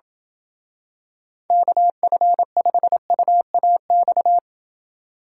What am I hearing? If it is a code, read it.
Morse code, heard as a keyed tone, about 27 wpm, 700 Hz. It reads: KF5UAX